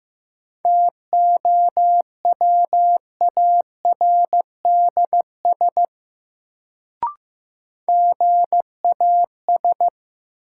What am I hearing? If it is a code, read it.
Morse code, heard as a keyed tone, about 15 wpm, 700 Hz. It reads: TOWARDS GAS
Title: Sentences from Top 1100 Words - New Word Drill Alt - 15wpm